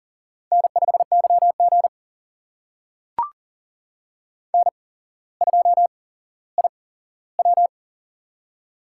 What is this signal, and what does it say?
Morse code, heard as a keyed tone, about 40 wpm, spaced as for 17 wpm, 700 Hz. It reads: N5YG N2IW
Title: USA Call Signs - 4 Characters - 8x Character Spacing - 40wpm